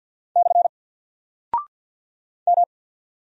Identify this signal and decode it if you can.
Morse code, heard as a keyed tone, about 50 wpm, 700 Hz. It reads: / M
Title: Single Character (Letter:Number:Punctuation:Prosign) 50wpm